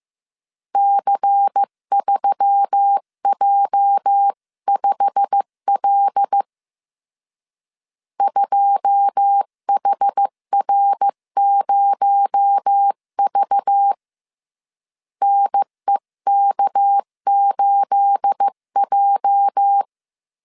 Morse code, heard as a keyed tone, about 15 wpm, 800 Hz. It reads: C3J5L 2HR0V NEK8J
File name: CW / DET / 1 9 1500C S B 14wpm 3302